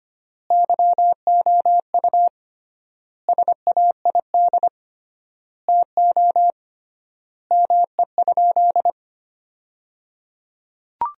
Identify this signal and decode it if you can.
Morse code, heard as a keyed tone, about 25 wpm, 700 Hz. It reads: YOU SAID TO ME?
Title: 3x Word-Spacing - Sentences from Top 200 Words - 25wpm